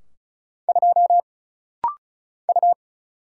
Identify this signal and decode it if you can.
Morse code, heard as a keyed tone, about 36 wpm, 700 Hz. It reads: 2 U
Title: Single Letter-Number Alt - Rapid-Fire 35wpm